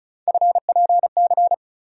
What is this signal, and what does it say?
Morse code, heard as a keyed tone, about 36 wpm, 700 Hz. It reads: FPC
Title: USA Call Sign Postfixes - 3 Letters 35wpm